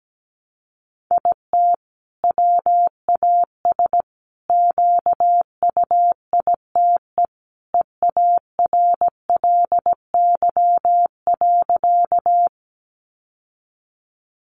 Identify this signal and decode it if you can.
Morse code, heard as a keyed tone, about 17 wpm, 700 Hz. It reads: IT WAS QUITE EARLY.
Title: Sentences from Top 700 Words 17wpm